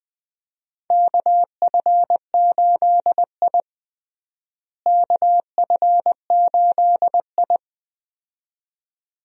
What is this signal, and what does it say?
Morse code, heard as a keyed tone, about 20 wpm, 700 Hz. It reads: KF8I KF8I
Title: USA Call Signs - 4 Characters - Repeated 2x - 20wpm